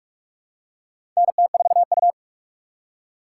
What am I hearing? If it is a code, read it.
Morse code, heard as a keyed tone, about 46 wpm, 700 Hz. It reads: NT4U
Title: USA Call Signs - 4 Characters - 3x Character Spacing - 45wpm